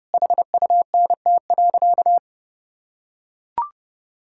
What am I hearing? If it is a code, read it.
Morse code, heard as a keyed tone, about 30 wpm, 700 Hz. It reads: HUNT.
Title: Sentences from Top 1600 Words 30wpm